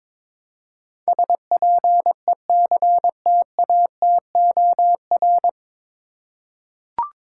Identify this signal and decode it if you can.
Morse code, heard as a keyed tone, about 22 wpm, 700 Hz. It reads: SPECTATOR
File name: practice / 22wpm / Nine Letter Words - 22wpm